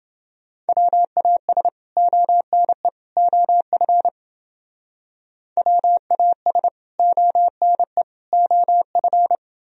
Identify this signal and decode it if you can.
Morse code, heard as a keyed tone, about 29 wpm, 700 Hz. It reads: WAS ONE OF WAS ONE OF